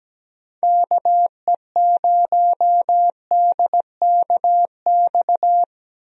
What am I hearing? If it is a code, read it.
Morse code, heard as a keyed tone, about 17 wpm, 700 Hz. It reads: KE0DKX
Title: USA Call Signs - 6 Characters - 17wpm